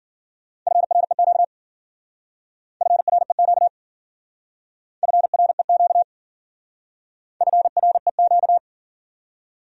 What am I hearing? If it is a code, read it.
Morse code, heard as a keyed tone, about 50 wpm, 700 Hz. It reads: FREQ FREQ FREQ FREQ